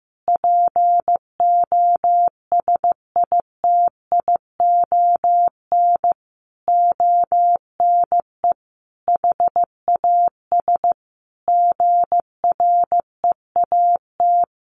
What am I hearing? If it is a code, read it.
Morse code, heard as a keyed tone, about 14 wpm, 700 Hz. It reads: POSITION ONE HAS GREAT
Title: Sentences from Top 1100 Words 15wpm